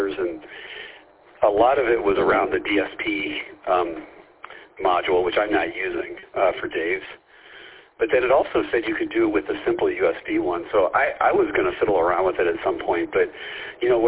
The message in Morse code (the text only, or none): none